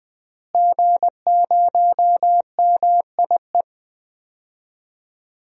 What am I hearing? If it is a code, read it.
Morse code, heard as a keyed tone, about 20 wpm, 700 Hz. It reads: G0MIE
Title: DXCC-100 Call Signs - 5 Characters - Repeated 3x - 20wpm